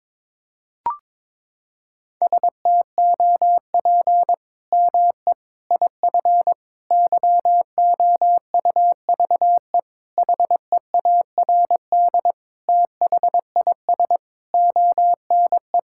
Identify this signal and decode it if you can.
Morse code, heard as a keyed tone, about 22 wpm, 700 Hz. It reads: STOP ME IF YOUVE HEARD THIS ONE
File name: practice / 22wpm / Sets of 8 Words - 22wpm